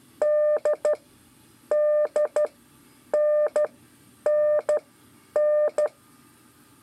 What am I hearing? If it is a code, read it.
Morse code, heard as a keyed tone, about 12 wpm, spaced as for 9 wpm, 600 Hz. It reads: D DNNN
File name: 5 LEZIONE RIPASSO 1